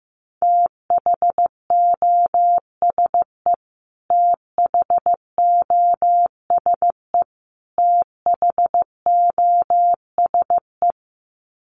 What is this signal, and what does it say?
Morse code, heard as a keyed tone, about 15 wpm, 700 Hz. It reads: THOSE THOSE THOSE
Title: Top 500 Words Repeated 15wpm